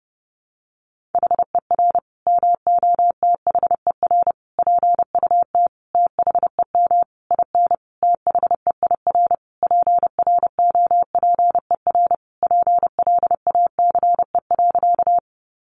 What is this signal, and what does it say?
Morse code, heard as a keyed tone, about 30 wpm, 700 Hz. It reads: HER MOTHER PUT THEM IN THEIR PROPER PLACE.
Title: Sentences from Top 800 Words 30wpm